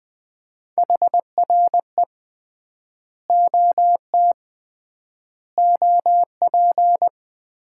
Morse code, heard as a keyed tone, about 20 wpm, 700 Hz. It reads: HRE OT OP
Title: Common QSO Phrases - 3x Word-Spacing - 20wpm